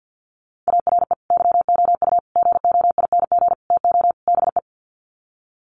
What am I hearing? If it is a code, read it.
Morse code, heard as a keyed tone, about 49 wpm, 700 Hz. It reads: ARE YOU GOING TO BE
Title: Sets of 5 Words Encore 50wpm